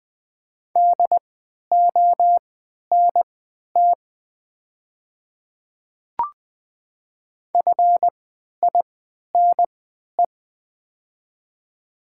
Morse code, heard as a keyed tone, about 20 wpm, spaced as for 11 wpm, 700 Hz. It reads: DONT FINE
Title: 3x Character Spacing - Top 500 Words - 20wpm